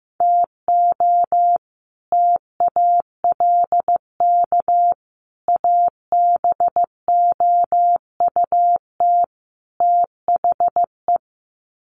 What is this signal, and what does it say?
Morse code, heard as a keyed tone, about 15 wpm, 700 Hz. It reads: TO TALK ABOUT THE
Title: Sets of 4 Words 15wpm